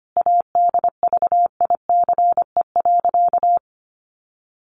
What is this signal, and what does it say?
Morse code, heard as a keyed tone, about 25 wpm, 700 Hz. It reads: ADVICE.
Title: Sentences from Top 1100 Words 25wpm